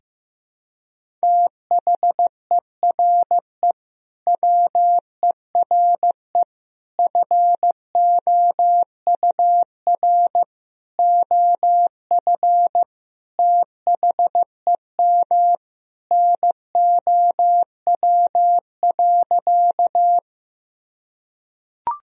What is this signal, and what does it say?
Morse code, heard as a keyed tone, about 16 wpm, 700 Hz. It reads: THERE WERE FOUR OF THEM NOW.
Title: Sentences from Top 1000 Words - Review Part 1 - 15wpm